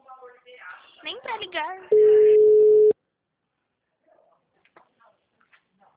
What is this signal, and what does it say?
Ringback tone (Europe, 425 Hz), with speech.